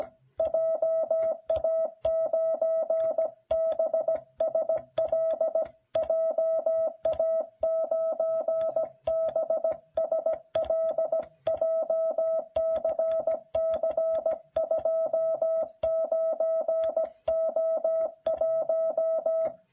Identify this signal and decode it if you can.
Morse code, heard as a keyed tone, about 17 wpm, 660 Hz. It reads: JA9BSL JA9BSL JCC29O1